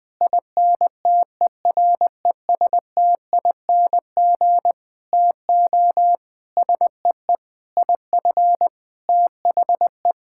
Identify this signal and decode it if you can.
Morse code, heard as a keyed tone, about 20 wpm, 700 Hz. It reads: INTERESTING TO SEE IF THE
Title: Sentences with 8-Word Phrases - 20wpm